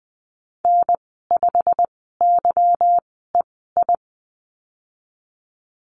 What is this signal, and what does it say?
Morse code, heard as a keyed tone, about 20 wpm, spaced as for 15 wpm, 700 Hz. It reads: N5YEI
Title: 173 Lesson 28g - Number 9 - Call Signs 20wpm